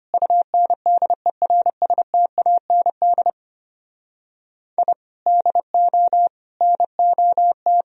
Morse code, heard as a keyed tone, about 31 wpm, 700 Hz. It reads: UNDERSTAND I DO NOT